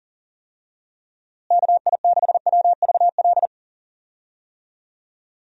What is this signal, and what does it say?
Morse code, heard as a keyed tone, about 40 wpm, 700 Hz. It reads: KI6WVL
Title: USA Call Signs - 6 Characters - Repeated 3x - 40wpm